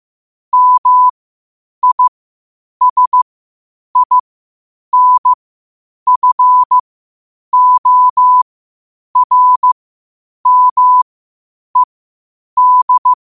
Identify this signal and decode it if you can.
Morse code, heard as a keyed tone, about 15 wpm, spaced as for 8 wpm, 1000 Hz. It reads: MISINFORMED
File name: qotdfiles / 8wpm